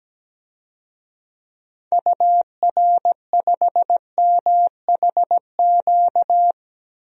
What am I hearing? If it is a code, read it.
Morse code, heard as a keyed tone, about 17 wpm, 700 Hz. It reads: UR5MHQ
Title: DXCC-100 Call Signs - 6 Characters - Repeated 3x - 17wpm